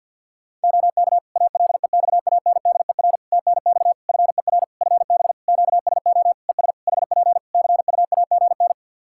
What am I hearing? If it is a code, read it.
Morse code, heard as a keyed tone, about 51 wpm, 700 Hz. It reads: OK ALEXANDER TNX FER FB QSO ES HP CUAGN
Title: K1USN SST QSO Practice 50wpm